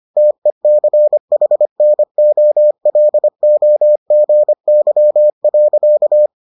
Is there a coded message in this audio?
Yes: Morse code from 0.2 to 6.3 s, heard as a keyed tone, about 25 wpm, 600 Hz. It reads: TECHNOLOGY.